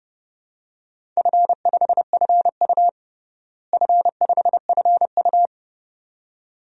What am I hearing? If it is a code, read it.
Morse code, heard as a keyed tone, about 32 wpm, 700 Hz. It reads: F5FU F5FU
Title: DXCC-100 Call Signs - 4 Characters - Repeated 2x - 30wpm